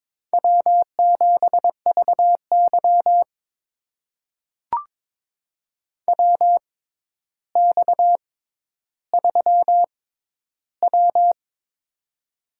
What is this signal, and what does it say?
Morse code, heard as a keyed tone, about 21 wpm, spaced as for 11 wpm, 700 Hz. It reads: W7VY WX3W
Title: USA Call Signs - 4 Characters - 6x Character Spacing - 22wpm